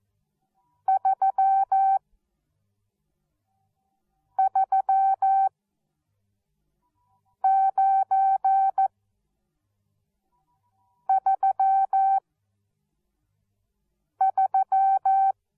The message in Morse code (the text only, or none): 33933